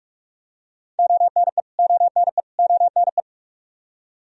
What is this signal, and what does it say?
Morse code, heard as a keyed tone, about 46 wpm, 700 Hz. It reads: ONE ONE ONE
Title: Top 100 Words Repeated 45wpm